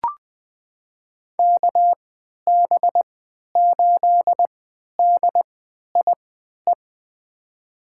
Morse code, heard as a keyed tone, about 20 wpm, spaced as for 11 wpm, 700 Hz. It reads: KB8DIE
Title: USA Call Signs - 6 Characters - 3x Character Spacing - 20wpm